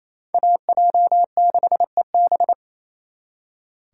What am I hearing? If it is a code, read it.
Morse code, heard as a keyed tone, about 35 wpm, 700 Hz. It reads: AJ6EB
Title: Speed Racing - USA 2x2 Call Signs 28wpm